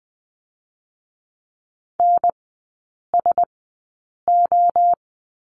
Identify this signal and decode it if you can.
Morse code, heard as a keyed tone, about 20 wpm, 700 Hz. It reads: N S O
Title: 035 Lesson 7e - Letter s - Three Characters 20wpm